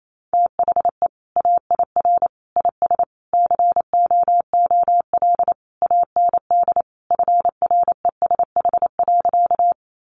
Morse code, heard as a keyed tone, about 28 wpm, 700 Hz. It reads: THE AIR IS COOL AND FRESH.
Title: Sentences from Top 1200 Words 28wpm